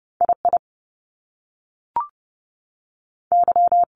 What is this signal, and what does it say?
Morse code, heard as a keyed tone, about 30 wpm, 700 Hz. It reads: I I Y
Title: Top Two Letter Combinations in Words 30wpm